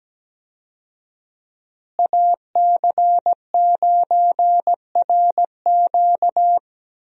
Morse code, heard as a keyed tone, about 17 wpm, 700 Hz. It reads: AC9RQ
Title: USA Call Signs - 5 Characters - Repeated 2x - 17wpm